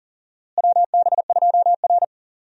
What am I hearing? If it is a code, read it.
Morse code, heard as a keyed tone, about 40 wpm, 700 Hz. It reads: WB2R
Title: USA Call Signs - 4 Characters - Repeated 3x - 40wpm